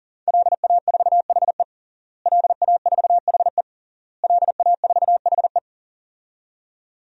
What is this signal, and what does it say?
Morse code, heard as a keyed tone, about 37 wpm, 700 Hz. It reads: LA4HE LA4HE LA4HE
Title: DXCC-100 Call Signs - 5 Characters - Repeated 3x - 40wpm